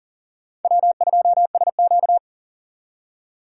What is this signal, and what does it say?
Morse code, heard as a keyed tone, about 52 wpm, 700 Hz. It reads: W2SQ